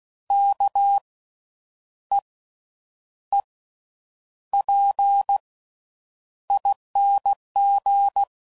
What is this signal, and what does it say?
Morse code, heard as a keyed tone, about 16 wpm, spaced as for 8 wpm, 800 Hz. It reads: KEEPING
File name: Suffixes / Special INGs / Slow With Faster Ings